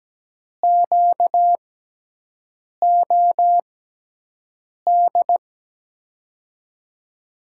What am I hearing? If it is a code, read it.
Morse code, heard as a keyed tone, about 17 wpm, spaced as for 6 wpm, 700 Hz. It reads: QOD
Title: USA Call Signs - 6x Character Spacing - 17wpm